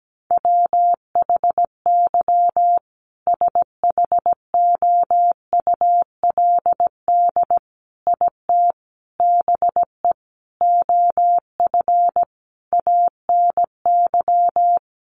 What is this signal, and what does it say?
Morse code, heard as a keyed tone, about 17 wpm, 700 Hz. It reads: WHY SHOULD IT BE OF ANY
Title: Sentences from Top 1200 Words 17wpm